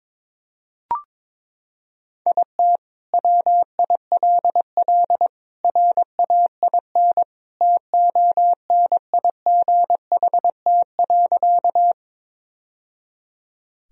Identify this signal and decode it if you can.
Morse code, heard as a keyed tone, about 22 wpm, 700 Hz. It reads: IT WILL RAIN TONIGHT.
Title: Sentences from Top 1000 Words - Review Part 1 - 22wpm